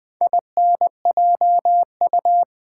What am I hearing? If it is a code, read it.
Morse code, heard as a keyed tone, about 20 wpm, 700 Hz. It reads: INJU
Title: Build A Word - 2 Letters At A Time - 6-Letter Words - 20wpm